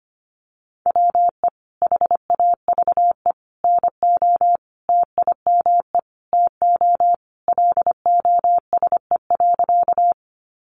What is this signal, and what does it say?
Morse code, heard as a keyed tone, about 25 wpm, 700 Hz. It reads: WE HAVE NO TIME TO LOSE.
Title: Sentences from Top 1100 Words 25wpm